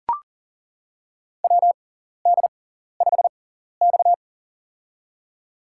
Morse code, heard as a keyed tone, about 40 wpm, spaced as for 14 wpm, 700 Hz. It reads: WD5X
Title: USA Call Signs - 4 Characters - 6x Character Spacing - 40wpm